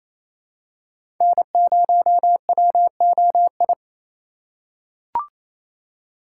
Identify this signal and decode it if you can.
Morse code, heard as a keyed tone, about 28 wpm, 700 Hz. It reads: N0WOI
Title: K1USN SST QSO Practice - 28wpm